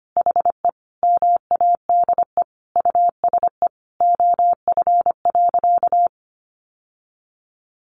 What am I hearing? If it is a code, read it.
Morse code, heard as a keyed tone, about 25 wpm, 700 Hz. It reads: HE MADE USE OF.
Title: Sentences from Top 1200 Words 25wpm